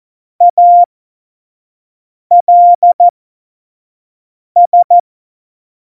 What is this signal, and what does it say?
Morse code, heard as a keyed tone, about 14 wpm, spaced as for 5 wpm, 700 Hz. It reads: ALS
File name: Morsekurs / Witze / Witz 21 Juli 2020, 25 F70